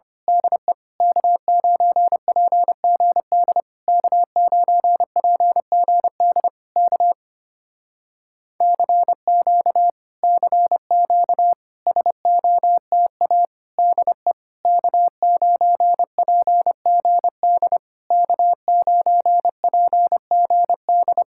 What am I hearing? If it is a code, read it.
Morse code, heard as a keyed tone, about 25 wpm, 700 Hz. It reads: DE K9PGD K9PGD K CQ CQ SOTA DE K9PGD K9PGD
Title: Speed-Racing Common QSO Phrases - 25wpm